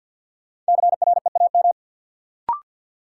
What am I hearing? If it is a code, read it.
Morse code, heard as a keyed tone, about 49 wpm, 700 Hz. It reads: CREAM
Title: Sentences from Top 1500 Words - New Word Drill Alt - 50wpm